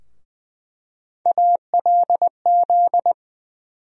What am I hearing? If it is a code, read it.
Morse code, heard as a keyed tone, about 20 wpm, 700 Hz. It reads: ALZ
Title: USA Call Sign Postfixes - 3 Letters 20wpm